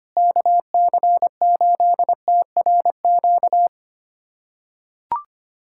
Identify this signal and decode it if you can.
Morse code, heard as a keyed tone, about 25 wpm, 700 Hz. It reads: KC8TRQ
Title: USA Callsign DE Callsign 25wpm